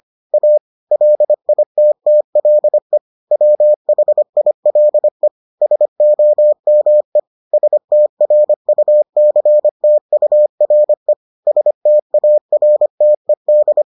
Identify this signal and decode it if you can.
Morse code, heard as a keyed tone, about 25 wpm, 600 Hz. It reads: A LITTLE WHILE SOME STRUCTURE STARTED